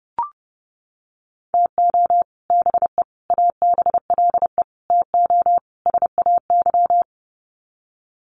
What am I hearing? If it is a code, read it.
Morse code, heard as a keyed tone, about 30 wpm, 700 Hz. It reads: TO BE ABLE TO SAY
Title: Sets of 5 Words Encore2 30wpm